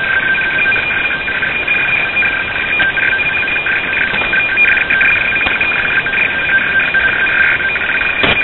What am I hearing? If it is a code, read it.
Two-tone teleprinter signal, about 850 Hz shift.